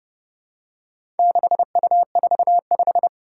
Morse code, heard as a keyed tone, about 31 wpm, 700 Hz. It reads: 6U45
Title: Four Challenging Letter-Number Combinations 30wpm